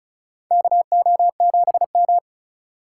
Morse code, heard as a keyed tone, about 36 wpm, 700 Hz. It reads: KO7M
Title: USA Call Signs - 4 Characters - 3x Character Spacing - 35wpm